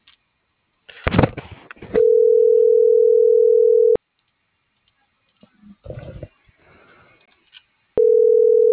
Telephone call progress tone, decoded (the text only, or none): ringback tone